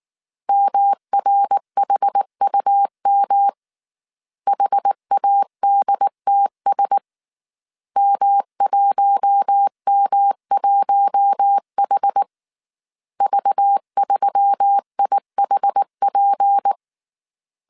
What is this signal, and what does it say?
Morse code, heard as a keyed tone, about 19 wpm, 800 Hz. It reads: MRHUM HADTS M1M1H V3IHP